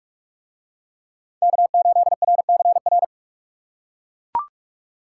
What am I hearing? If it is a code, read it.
Morse code, heard as a keyed tone, about 41 wpm, 700 Hz. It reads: K8RCR